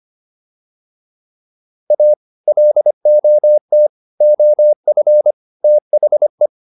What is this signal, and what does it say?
Morse code, heard as a keyed tone, about 25 wpm, 600 Hz. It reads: A LOT OF THE